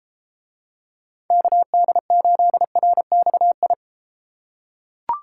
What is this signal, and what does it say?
Morse code, heard as a keyed tone, about 30 wpm, 700 Hz. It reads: KD8RXI